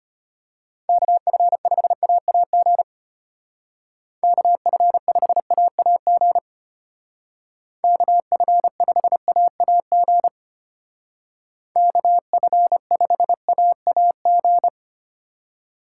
Morse code, heard as a keyed tone, about 30 wpm, 700 Hz. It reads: KF5AAG KF5AAG KF5AAG KF5AAG
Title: Speed Racing - USA 2x3 Call Signs 25wpm